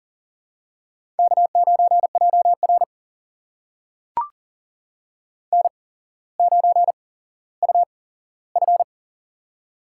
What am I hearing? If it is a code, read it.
Morse code, heard as a keyed tone, about 40 wpm, spaced as for 17 wpm, 700 Hz. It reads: K9JR N9UF
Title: USA Call Signs - 4 Characters - 8x Character Spacing - 40wpm